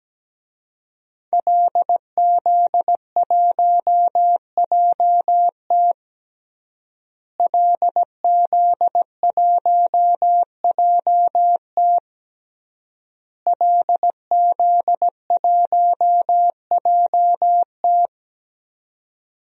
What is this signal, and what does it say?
Morse code, heard as a keyed tone, about 17 wpm, 700 Hz. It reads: LZ1JT LZ1JT LZ1JT
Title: DXCC-100 Call Signs - 5 Characters - Repeated 3x - 17wpm